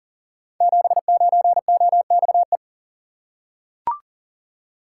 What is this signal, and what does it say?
Morse code, heard as a keyed tone, about 41 wpm, 700 Hz. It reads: 79OXE
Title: Code Groups - 5 Letter:Numbers - 40wpm